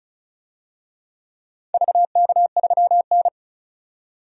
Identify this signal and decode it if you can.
Morse code, heard as a keyed tone, about 36 wpm, 700 Hz. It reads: VK3N
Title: DXCC-100 Call Signs - 4 Characters - 35wpm